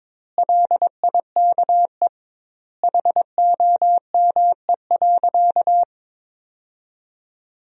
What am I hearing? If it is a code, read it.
Morse code, heard as a keyed tone, about 22 wpm, 700 Hz. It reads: LIKE HOME.
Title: 2x Word-Spacing - Sentences from Top 200 Words - 22wpm